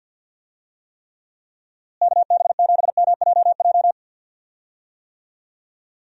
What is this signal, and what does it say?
Morse code, heard as a keyed tone, about 51 wpm, 700 Hz. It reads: KB7MJJ
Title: USA Call Signs - 6 Characters - Repeated 3x - 50wpm